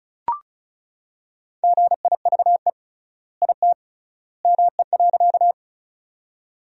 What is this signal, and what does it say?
Morse code, heard as a keyed tone, about 36 wpm, 700 Hz. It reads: GIVE IT ME.
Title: 3x Word-Spacing - Sentences from Top 200 Words - 35wpm